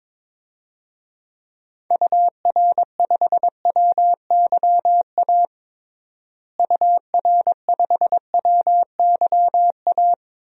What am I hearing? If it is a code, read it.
Morse code, heard as a keyed tone, about 21 wpm, 700 Hz. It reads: UR5WYA UR5WYA
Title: DXCC-100 Call Signs - 6 Characters - Repeated 2x - 22wpm